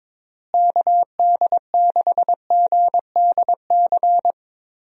Morse code, heard as a keyed tone, about 22 wpm, 700 Hz. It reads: KD6GDC
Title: USA Call Signs - 6 Characters - Repeated 2x - 22wpm